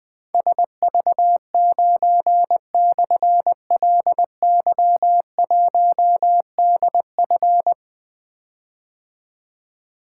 Morse code, heard as a keyed tone, about 20 wpm, 700 Hz. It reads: SV9/LY1DF
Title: Difficult Call Signs 20wpm